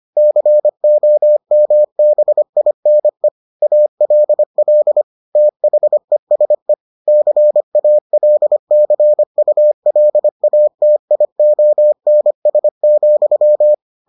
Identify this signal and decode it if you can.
Morse code, heard as a keyed tone, about 25 wpm, 600 Hz. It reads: COMBINE ALL THESE CALCULATIONS,